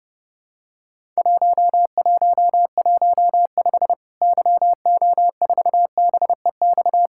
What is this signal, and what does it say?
Morse code, heard as a keyed tone, about 30 wpm, 700 Hz. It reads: 1115 YO4BEX
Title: Worked All Europe DX Contest - QTC Exchange - 30wpm